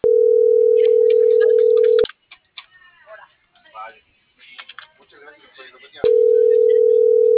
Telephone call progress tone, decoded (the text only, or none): ringback tone